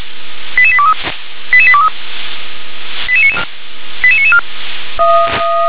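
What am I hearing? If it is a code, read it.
Five-tone selective call (ZVEI), heard as quick runs of five tones, tones of about 70 ms. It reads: 88823 88823 88994
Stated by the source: ZVEI five-tone selective call 88979, received via SDR.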